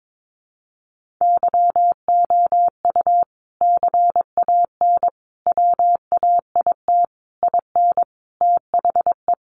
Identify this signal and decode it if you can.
Morse code, heard as a keyed tone, about 22 wpm, 700 Hz. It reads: YOU CAN WAIT IN THE
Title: Sentences from Top 1400 Words 22wpm